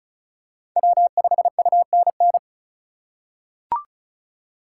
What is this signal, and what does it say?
Morse code, heard as a keyed tone, about 34 wpm, 700 Hz. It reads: W5UNN
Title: USA Call Signs - 5 Characters - 4x Character Spacing - 35wpm